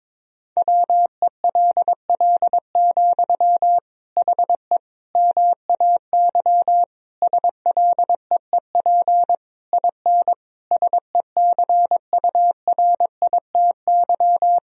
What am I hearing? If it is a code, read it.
Morse code, heard as a keyed tone, about 21 wpm, 700 Hz. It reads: WELL, HE MAY SLEEP IN SECURITY
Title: Sentences from Top 1000 Words - Review Part 4 - 22wpm